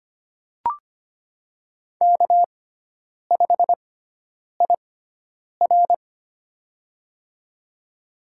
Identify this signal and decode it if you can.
Morse code, heard as a keyed tone, about 25 wpm, spaced as for 9 wpm, 700 Hz. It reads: K5IR
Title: USA Call Signs - 4 Characters - 6x Character Spacing - 25wpm